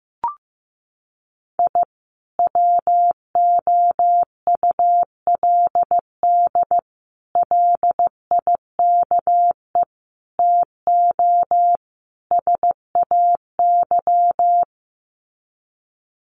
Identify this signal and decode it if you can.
Morse code, heard as a keyed tone, about 15 wpm, 700 Hz. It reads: I WOULD LIKE TO SAY
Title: Sets of 5 Words Encore2 15wpm